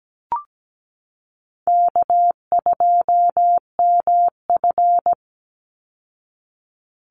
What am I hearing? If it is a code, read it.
Morse code, heard as a keyed tone, about 17 wpm, 700 Hz. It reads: K2MF
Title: Easy Call Signs Repeated 2x - 17wpm